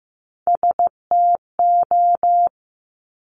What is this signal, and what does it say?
Morse code, heard as a keyed tone, about 15 wpm, 700 Hz. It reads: STO
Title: Top Three Letter Combinations in Words 15wpm